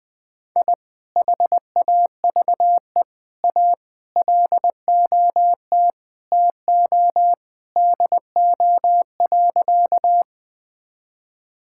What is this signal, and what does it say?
Morse code, heard as a keyed tone, about 21 wpm, 700 Hz. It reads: I HAVE A LOT TO DO.
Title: Sentences with 2-Word Phrases - 20wpm